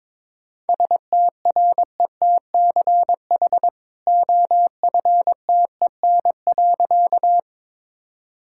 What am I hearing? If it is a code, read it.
Morse code, heard as a keyed tone, about 22 wpm, 690 Hz. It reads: STRETCH OFTEN.
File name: practice / 22wpm / Easy Sentences with Hints - 22wpm